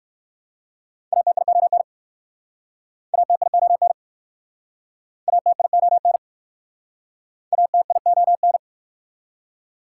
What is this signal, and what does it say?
Morse code, heard as a keyed tone, about 59 wpm, 700 Hz. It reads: ATION ATION ATION ATION